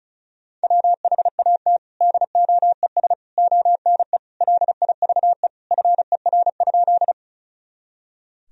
Morse code, heard as a keyed tone, about 34 wpm, 700 Hz. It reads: WHAT DOES ONE LIVE FER?
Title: Sentences from Top 200 Words and Abbreviations 35wpm